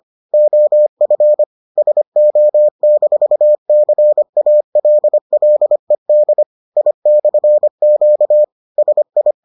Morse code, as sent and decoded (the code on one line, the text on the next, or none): --- ..-. / ... --- -....- -.-. .- .-.. .-.. -.. / .. -..-. --.- / ... ..
OF SO-CALLED I/Q SI